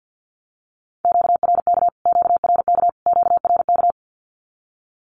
Morse code, heard as a keyed tone, about 48 wpm, 700 Hz. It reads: QRK QRK QRK
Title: Q-Codes Repeated 3x 50wpm